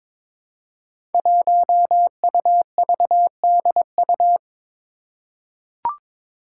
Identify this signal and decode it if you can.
Morse code, heard as a keyed tone, about 22 wpm, 700 Hz. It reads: 1UVDU
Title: Five Challenging Letter-Number Combinations 22wpm